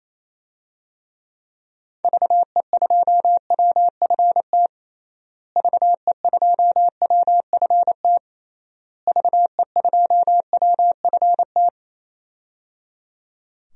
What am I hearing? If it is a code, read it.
Morse code, heard as a keyed tone, about 28 wpm, 700 Hz. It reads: VE2WFT VE2WFT VE2WFT
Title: DXCC-100 Call Signs - 6 Characters - Repeated 3x - 28wpm